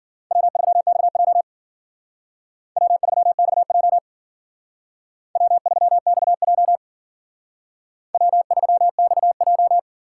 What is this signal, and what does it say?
Morse code, heard as a keyed tone, about 52 wpm, 700 Hz. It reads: W3XJ W3XJ W3XJ W3XJ